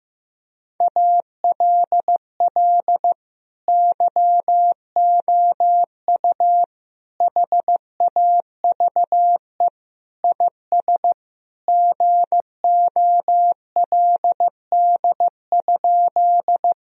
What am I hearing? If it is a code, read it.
Morse code, heard as a keyed tone, about 15 wpm, 700 Hz. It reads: ALL YOU HAVE IS GOLD?
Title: Sentences from Top 1000 Words - Review Part 3 - 15wpm